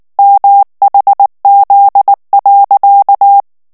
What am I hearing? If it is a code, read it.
Morse code, heard as a keyed tone, about 19 wpm, 800 Hz. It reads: MHZ.
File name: ANGLAIS texte clair 3 vitesse 18 mots minute